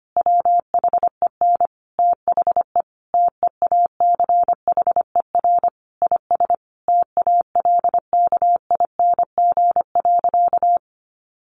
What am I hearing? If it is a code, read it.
Morse code, heard as a keyed tone, about 25 wpm, 700 Hz. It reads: WHEN THE TEACHER IS TALKING.
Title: Additional Easy Phrases 25wpm